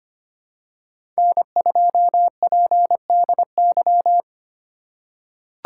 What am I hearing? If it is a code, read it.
Morse code, heard as a keyed tone, about 25 wpm, 700 Hz. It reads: N2PDY
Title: USA Call Signs - 5 Characters - 2x Character Spacing - 25wpm